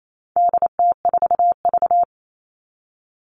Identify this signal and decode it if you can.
Morse code, heard as a keyed tone, about 28 wpm, 700 Hz. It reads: DT4V
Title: All Cut Numbers - 4-Digit 28wpm